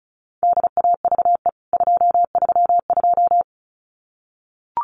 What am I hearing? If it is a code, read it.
Morse code, heard as a keyed tone, about 34 wpm, 700 Hz. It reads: DAVE 232
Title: CWT Exchange 35wpm